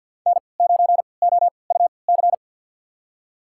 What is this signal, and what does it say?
Morse code, heard as a keyed tone, about 48 wpm, spaced as for 28 wpm, 700 Hz. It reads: N9OUC
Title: USA Call Signs - 3x Character Spacing - 50wpm